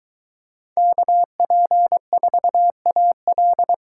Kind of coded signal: Morse code

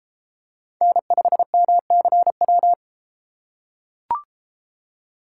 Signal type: Morse code

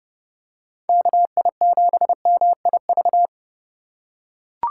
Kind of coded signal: Morse code

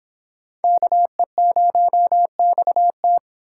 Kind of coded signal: Morse code

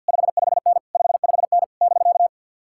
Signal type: Morse code

